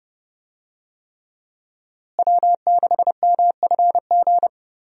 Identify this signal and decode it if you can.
Morse code, heard as a keyed tone, about 32 wpm, 700 Hz. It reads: W6MFG
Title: USA Call Signs - 5 Characters - Repeated 3x - 30wpm